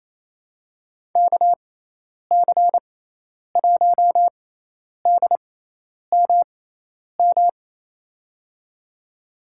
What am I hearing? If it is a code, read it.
Morse code, heard as a keyed tone, about 28 wpm, spaced as for 10 wpm, 700 Hz. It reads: KC1DMM
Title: USA Call Signs - 6 Characters - 6x Character Spacing - 28wpm